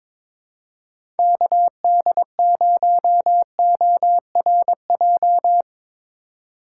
Morse code, heard as a keyed tone, about 22 wpm, 700 Hz. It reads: KD0ORJ
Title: USA Call Signs - 6 Characters - 4x Character Spacing - 22wpm